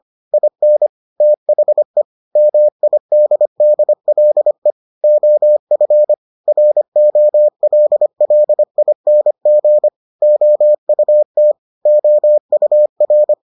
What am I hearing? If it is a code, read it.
Morse code, heard as a keyed tone, about 25 wpm, 600 Hz. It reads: IN THE MIDDLE OF ROLLING OUT OUR